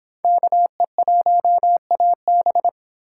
Morse code, heard as a keyed tone, about 25 wpm, 700 Hz. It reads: KE1AB